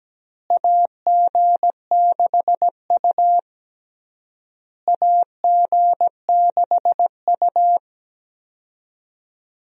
Morse code, heard as a keyed tone, about 17 wpm, 700 Hz. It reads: AG6U AG6U
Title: USA Call Signs - 4 Characters - Repeated 3x - 17wpm